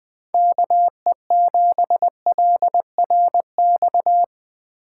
Morse code, heard as a keyed tone, about 22 wpm, 700 Hz. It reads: KE7LRX
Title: Speed Racing - USA 2x3 Call Signs 20wpm